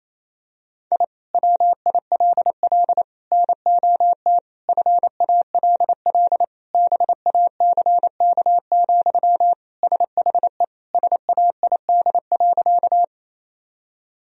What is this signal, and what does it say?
Morse code, heard as a keyed tone, about 28 wpm, 700 Hz. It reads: I WILL NOT FALL BACK, SHE SAID.
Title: Sentences from Top 1000 Words - Review Part 2 - 28wpm